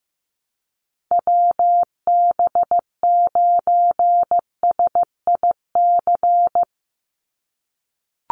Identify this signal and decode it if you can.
Morse code, heard as a keyed tone, about 15 wpm, 700 Hz. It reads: WB9SIC
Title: USA Call Signs 15wpm